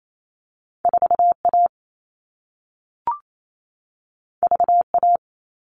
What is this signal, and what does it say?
Morse code, heard as a keyed tone, about 28 wpm, 700 Hz. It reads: 4A VA
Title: All Cut Numbers - 2-Digit 28wpm